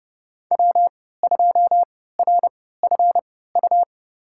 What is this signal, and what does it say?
Morse code, heard as a keyed tone, about 31 wpm, spaced as for 17 wpm, 700 Hz. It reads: W2RFU
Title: USA Call Signs - 5 Characters - 3x Character Spacing - 30wpm